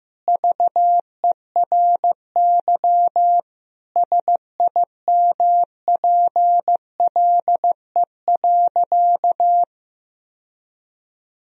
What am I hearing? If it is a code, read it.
Morse code, heard as a keyed tone, about 16 wpm, 700 Hz. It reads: VERY SIMPLE.
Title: Sentences from Top 1000 Words - Review Part 4 - 15wpm